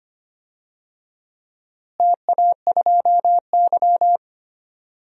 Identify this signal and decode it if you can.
Morse code, heard as a keyed tone, about 25 wpm, 700 Hz. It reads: TA2Y
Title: DXCC-100 Call Signs - 4 Characters - Repeated 2x - 25wpm